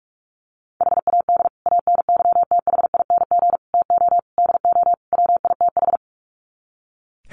Morse code, heard as a keyed tone, about 44 wpm, 700 Hz. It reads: HAD ANYTHING TO DO WITH
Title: Sets of 5 Words Encore 45wpm